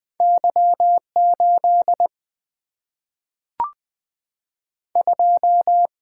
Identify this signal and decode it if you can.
Morse code, heard as a keyed tone, about 20 wpm, 700 Hz. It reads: Y8 2
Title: Single-Letter Single-Number 20wpm